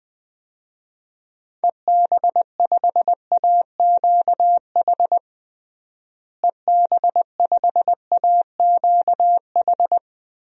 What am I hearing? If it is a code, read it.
Morse code, heard as a keyed tone, about 21 wpm, 700 Hz. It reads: EB5AQH EB5AQH
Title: DXCC-100 Call Signs - 6 Characters - Repeated 3x - 20wpm